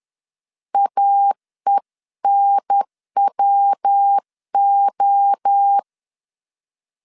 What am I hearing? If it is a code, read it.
Morse code, heard as a keyed tone, about 10 wpm, 800 Hz. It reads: AENWO
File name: CW / DET / 1 3 1500C S B 10wpm 3335